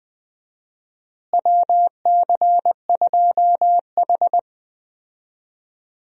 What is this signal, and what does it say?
Morse code, heard as a keyed tone, about 20 wpm, 700 Hz. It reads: WC2H
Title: USA Call Signs - 4 Characters - Repeated 2x - 20wpm